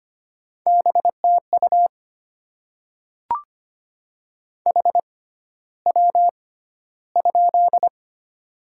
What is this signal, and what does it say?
Morse code, heard as a keyed tone, about 25 wpm, spaced as for 13 wpm, 700 Hz. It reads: BTU HW?
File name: Top 20 QSO Elements - 6x Character Spacing - 25wpm